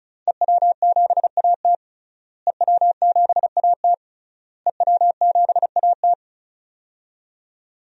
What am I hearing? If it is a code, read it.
Morse code, heard as a keyed tone, about 36 wpm, 700 Hz. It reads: EW7AT EW7AT EW7AT
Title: DXCC-100 Call Signs - 5 Characters - Repeated 3x - 35wpm